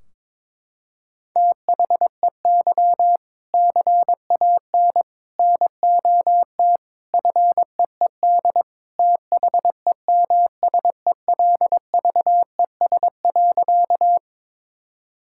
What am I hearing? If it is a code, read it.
Morse code, heard as a keyed tone, about 22 wpm, 700 Hz. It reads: THEY CAN NOT FEED THEMSELVES.
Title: Sentences from Top 1600 Words 22wpm